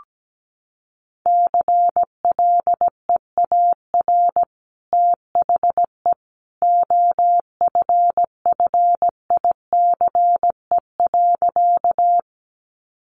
Morse code, heard as a keyed tone, about 17 wpm, 700 Hz. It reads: CLEAR THE OFFICE.
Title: Sentences from Top 1000 Words 17wpm